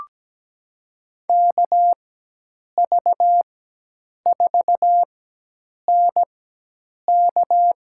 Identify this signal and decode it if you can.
Morse code, heard as a keyed tone, about 17 wpm, spaced as for 8 wpm, 700 Hz. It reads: KV4NK
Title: USA Call Signs - 4x Character Spacing - 17wpm